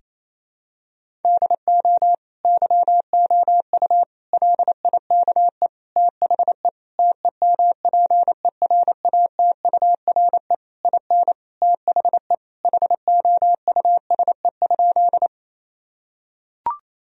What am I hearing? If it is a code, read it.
Morse code, heard as a keyed tone, about 27 wpm, 700 Hz. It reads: DO YOU LIKE THE TEMPERATURE IN THE HOUSE?
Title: Sentences from Top 1000 Words - Review Part 4 - 28wpm